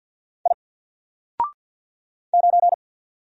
Morse code, heard as a keyed tone, about 48 wpm, 700 Hz. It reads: I 9
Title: Single Character (Letter:Number:Punctuation:Prosign) Alt 50wpm